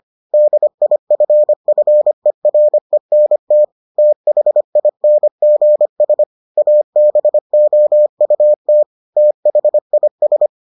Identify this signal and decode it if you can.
Morse code, heard as a keyed tone, about 25 wpm, 600 Hz. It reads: DIFFERENT THINGS ABOUT THIS